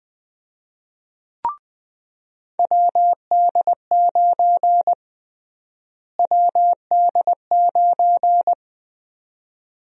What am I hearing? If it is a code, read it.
Morse code, heard as a keyed tone, about 20 wpm, 700 Hz. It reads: WD9 WD9
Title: USA Call Sign Prefixes - Repeated 2x - 20wpm